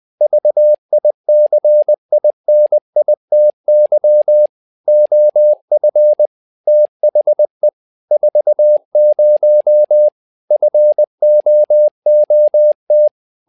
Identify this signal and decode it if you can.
Morse code, heard as a keyed tone, about 20 wpm, 600 Hz. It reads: VICINITY OF THE 40 FOOT
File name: tower trouble 20wpm